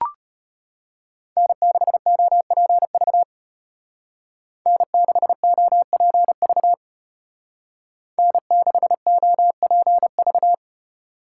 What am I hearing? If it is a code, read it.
Morse code, heard as a keyed tone, about 33 wpm, 700 Hz. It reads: N6OPV N6OPV N6OPV